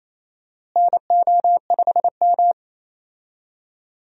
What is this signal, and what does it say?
Morse code, heard as a keyed tone, about 27 wpm, 700 Hz. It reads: NO5M
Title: USA Call Signs - 4 Characters - Repeated 3x - 28wpm